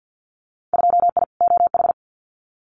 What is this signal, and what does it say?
Morse code, heard as a keyed tone, about 50 wpm, 700 Hz. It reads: 2I OH
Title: Winter Field Day Exchanges 50wpm